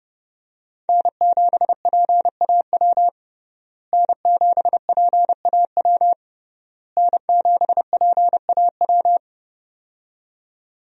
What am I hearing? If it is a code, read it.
Morse code, heard as a keyed tone, about 31 wpm, 700 Hz. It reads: N7PAW N7PAW N7PAW
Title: USA Call Signs - 5 Characters - Repeated 3x - 30wpm